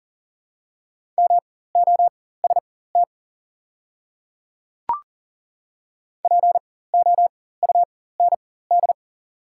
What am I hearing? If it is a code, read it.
Morse code, heard as a keyed tone, about 40 wpm, spaced as for 19 wpm, 690 Hz. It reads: MOST POUND
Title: 4x Character Spacing - Top 500 Words - 40wpm